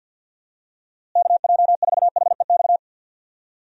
Morse code, heard as a keyed tone, about 50 wpm, 700 Hz. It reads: KJ4HEX
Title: USA Call Signs - 6 Characters - Repeated 2x - 50wpm